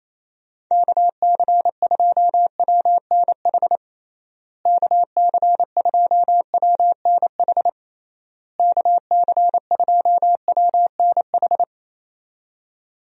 Morse code, heard as a keyed tone, about 27 wpm, 700 Hz. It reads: KC2WNH KC2WNH KC2WNH
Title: USA Call Signs - 6 Characters - Repeated 3x - 28wpm